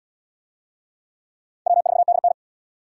Morse code, heard as a keyed tone, about 66 wpm, 700 Hz. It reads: W4DN